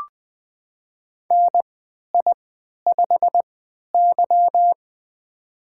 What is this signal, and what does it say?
Morse code, heard as a keyed tone, about 20 wpm, spaced as for 11 wpm, 700 Hz. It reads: NI5Y
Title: USA Call Signs - 4 Characters - 3x Character Spacing - 20wpm